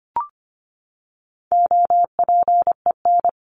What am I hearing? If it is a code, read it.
Morse code, heard as a keyed tone, about 25 wpm, 700 Hz. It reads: OPEN
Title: Top 300 Words 25wpm